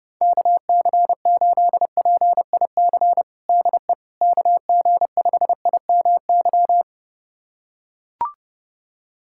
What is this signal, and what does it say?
Morse code, heard as a keyed tone, about 32 wpm, 700 Hz. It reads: KC8PIC DE KG5IMY
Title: USA Callsign DE Callsign - 6 Characters - 30wpm